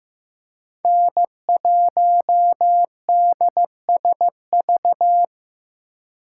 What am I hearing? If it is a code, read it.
Morse code, heard as a keyed tone, about 18 wpm, 700 Hz. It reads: N1DSV
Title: Speed Racing - USA 1x3 Call Signs 15wpm